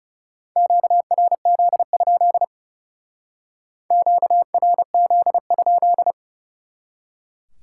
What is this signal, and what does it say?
Morse code, heard as a keyed tone, about 38 wpm, 700 Hz. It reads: QRZ? QRZ?